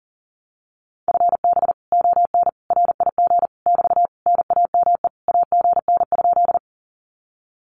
Morse code, heard as a keyed tone, about 40 wpm, 700 Hz. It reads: FB ON RIG <BT> NAME AGN?
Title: Common QSO Phrases Encore 40wpm